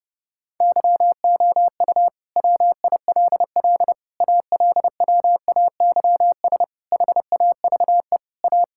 Morse code, heard as a keyed tone, about 30 wpm, 700 Hz. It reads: YOU WILL ALWAYS HAVE A
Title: Sentences from Top 1600 Words 30wpm